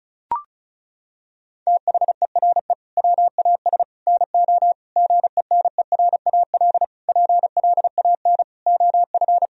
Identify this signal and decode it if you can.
Morse code, heard as a keyed tone, about 34 wpm, 700 Hz. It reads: THERE WAS NO GENERAL PLAN OF
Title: Sentences from Top 1000 Words - Review Part 4 - 35wpm